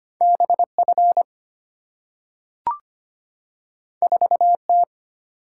Morse code, heard as a keyed tone, about 25 wpm, 700 Hz. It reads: BF 4T
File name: Two Letter-Number Combinations - 25wpm